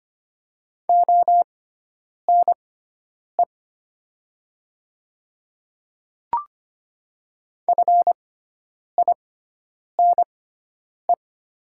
Morse code, heard as a keyed tone, about 25 wpm, spaced as for 9 wpm, 700 Hz. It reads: ONE FINE
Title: 6x Character Spacing - Top 500 Words - 25wpm